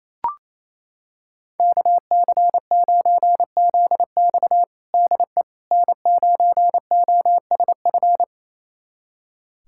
Morse code, heard as a keyed tone, about 28 wpm, 700 Hz. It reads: KC9ZX DE N9OSF
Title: USA Callsign DE Callsign - 5 Characters - 28wpm